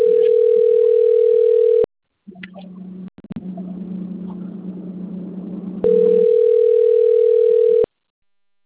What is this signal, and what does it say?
Ringback tone (North America), 440 + 480 Hz.
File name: rg-1600-unknown- - - .64953